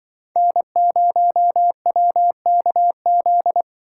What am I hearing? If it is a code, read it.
Morse code, heard as a keyed tone, about 25 wpm, 700 Hz. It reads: N0WKZ
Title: Speed Racing - USA 1x3 Call Signs 20wpm